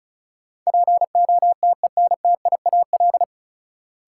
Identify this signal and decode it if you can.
Morse code, heard as a keyed tone, about 34 wpm, 700 Hz. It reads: POTENTIAL
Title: Sentences from Top 1000 Words - Review Part 4 - New Word Drill - 35wpm